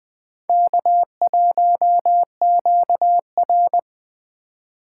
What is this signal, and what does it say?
Morse code, heard as a keyed tone, about 20 wpm, 700 Hz. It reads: K1QR